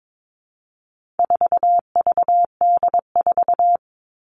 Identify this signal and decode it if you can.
Morse code, heard as a keyed tone, about 22 wpm, 700 Hz. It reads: 4VD4
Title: All Cut Numbers - 4-Digit 22wpm